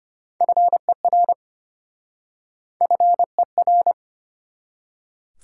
Morse code, heard as a keyed tone, about 34 wpm, 700 Hz. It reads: FER FER